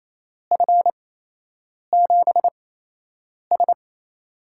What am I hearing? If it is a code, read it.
Morse code, heard as a keyed tone, about 28 wpm, spaced as for 8 wpm, 700 Hz. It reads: F7S
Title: USA Call Signs - 8x Character Spacing - 28wpm